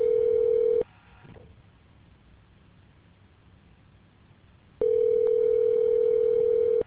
Ringback tone (North America), 440 + 480 Hz.